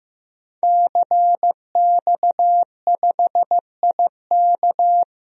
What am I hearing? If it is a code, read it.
Morse code, heard as a keyed tone, about 15 wpm, 700 Hz. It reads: CX5IK
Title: DXCC-100 Call Signs - 5 Characters - Repeated 2x - 15wpm